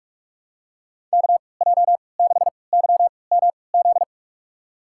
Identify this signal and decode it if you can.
Morse code, heard as a keyed tone, about 46 wpm, spaced as for 25 wpm, 700 Hz. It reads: KJ6YMZ
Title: USA Call Signs - 3x Character Spacing - 45wpm